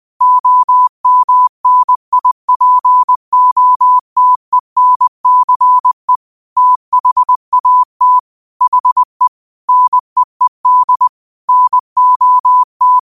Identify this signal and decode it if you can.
Morse code, heard as a keyed tone, about 20 wpm, 1000 Hz. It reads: OMNIPOTENCE THAT HE NEED NOT